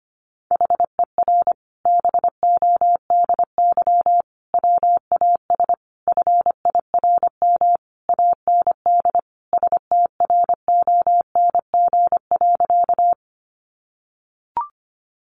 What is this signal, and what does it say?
Morse code, heard as a keyed tone, about 25 wpm, 700 Hz. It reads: HER BODY WAS FIRM AND STRONG.
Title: Sentences from Top 1400 Words 25wpm